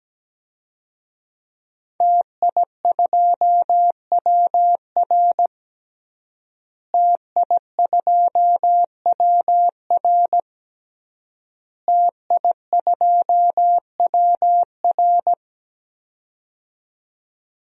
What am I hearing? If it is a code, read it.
Morse code, heard as a keyed tone, about 17 wpm, 700 Hz. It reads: TI2WR TI2WR TI2WR
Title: DXCC-100 Call Signs - 5 Characters - Repeated 3x - 17wpm